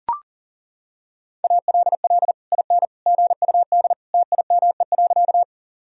Morse code, heard as a keyed tone, about 40 wpm, 700 Hz. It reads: ALL IN GUD TIME.
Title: Sentences from Top 200 Words and Abbreviations 40wpm